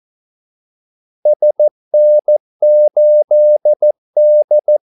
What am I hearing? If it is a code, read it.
Morse code, heard as a keyed tone, about 14 wpm, 600 Hz. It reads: SN8D
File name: Callsigns-14wpm